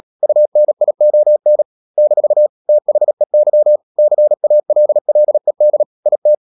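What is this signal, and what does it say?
Morse code, heard as a keyed tone, about 37 wpm, 600 Hz. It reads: UNION - THEY CALLED IT
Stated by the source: Morse code practice at 35–39 wpm sending English text of Ken Follett's novel Never, from CW med Gnister.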